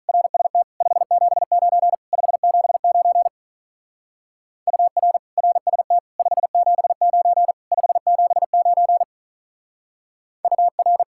Morse code, heard as a keyed tone, about 39 wpm, 700 Hz. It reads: RST 579 579 UR RST 579 579 UR
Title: Speed-Racing Common QSO Phrases - 35wpm